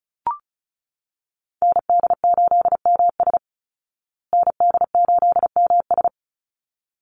Morse code, heard as a keyed tone, about 34 wpm, 700 Hz. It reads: ND8MS ND8MS
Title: Call Signs Repeated 3x - 35wpm